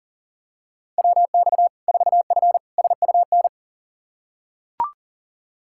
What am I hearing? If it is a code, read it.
Morse code, heard as a keyed tone, about 40 wpm, 700 Hz. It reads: WX 4F SUN
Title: Practice QSO - Exchange 2 - Info - 40wpm